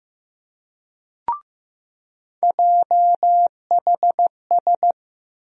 Morse code, heard as a keyed tone, about 15 wpm, 700 Hz. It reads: JHS
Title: USA Call Sign Postfixes - 3 Letters 15wpm